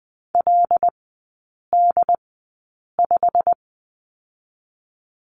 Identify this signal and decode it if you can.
Morse code, heard as a keyed tone, about 20 wpm, spaced as for 8 wpm, 700 Hz. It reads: LD5
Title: 087 Lesson 15e - Number 5 - Three Characters 20wpm